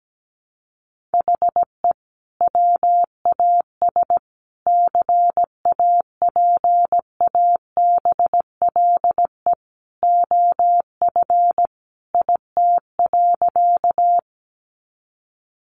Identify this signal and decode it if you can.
Morse code, heard as a keyed tone, about 17 wpm, 700 Hz. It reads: HE WAS CAPABLE OF IT.
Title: Sentences from Top 1200 Words 17wpm